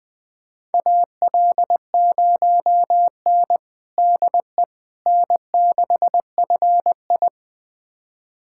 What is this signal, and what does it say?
Morse code, heard as a keyed tone, about 20 wpm, 700 Hz. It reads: AL0N DE N6FI
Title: USA Callsign DE Callsign - 4 Characters - 20wpm